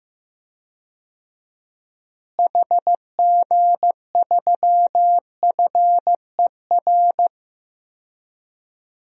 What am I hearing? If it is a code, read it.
Morse code, heard as a keyed tone, about 15 wpm, 700 Hz. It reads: HG3FER
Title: DXCC-100 Call Signs - 6 Characters - Repeated 2x - 15wpm